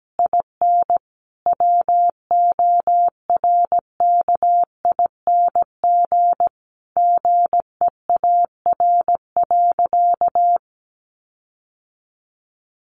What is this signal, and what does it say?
Morse code, heard as a keyed tone, about 17 wpm, 700 Hz. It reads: IN WORKING GEAR.
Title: Sentences from Top 1400 Words 17wpm